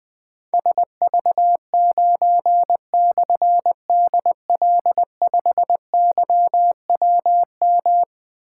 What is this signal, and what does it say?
Morse code, heard as a keyed tone, about 20 wpm, 700 Hz. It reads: SV9/DL5YWM
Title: Difficult Call Signs 20wpm